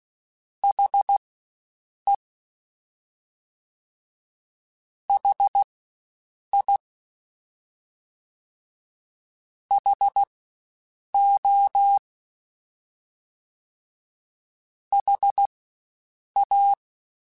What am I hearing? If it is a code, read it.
Morse code, heard as a keyed tone, about 16 wpm, spaced as for 7 wpm, 800 Hz. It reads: HE HI HO HA
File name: Spaced Alphas / Spaced Alphas mp3s / Hs sp @15